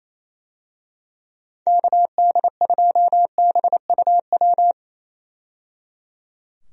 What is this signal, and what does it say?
Morse code, heard as a keyed tone, about 28 wpm, 700 Hz. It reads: KD2BUW
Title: USA Call Signs - 6 Characters - 28wpm